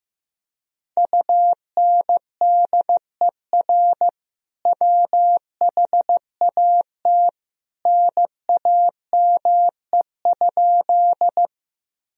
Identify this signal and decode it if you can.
Morse code, heard as a keyed tone, about 15 wpm, 700 Hz. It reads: UNDER WHAT NAME?
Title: Sentences from Top 1000 Words - Review Part 1 - 15wpm